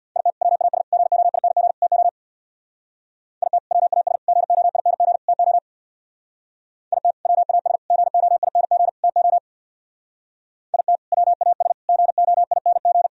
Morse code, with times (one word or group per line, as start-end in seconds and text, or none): 0.2-0.3 IT
0.4-0.8 WAS
0.9-1.7 GOING
1.8-2.1 TO
3.4-3.6 IT
3.7-4.2 WAS
4.3-5.2 GOING
5.3-5.6 TO
6.9-7.1 IT
7.3-7.8 WAS
7.9-8.9 GOING
9.0-9.4 TO
10.7-11.0 IT
11.1-11.7 WAS
11.9-13.1 GOING